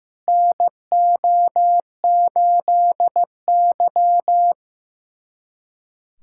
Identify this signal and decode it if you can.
Morse code, heard as a keyed tone, about 15 wpm, 700 Hz. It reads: NO8Y
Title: USA Callsign DE Callsign - 4 Characters - 15wpm